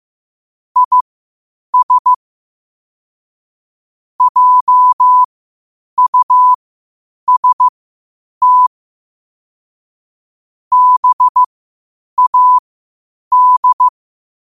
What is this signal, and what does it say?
Morse code, heard as a keyed tone, about 15 wpm, spaced as for 8 wpm, 1000 Hz. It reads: IS JUST BAD
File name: qotdfiles / 8wpm